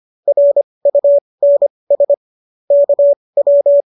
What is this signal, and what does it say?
Morse code, heard as a keyed tone, about 25 wpm, spaced as for 20 wpm, 570 Hz. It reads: RUNS KW